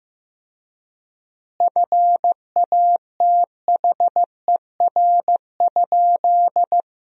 Morse code, heard as a keyed tone, about 15 wpm, 700 Hz. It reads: FATHER?
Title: 3x Word-Spacing - Sentences from Top 300 Words - 15wpm